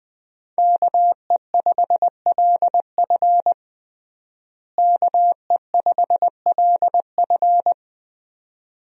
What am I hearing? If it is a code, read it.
Morse code, heard as a keyed tone, about 20 wpm, 700 Hz. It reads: KE5LF KE5LF
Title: USA Call Signs - 5 Characters - Repeated 2x - 20wpm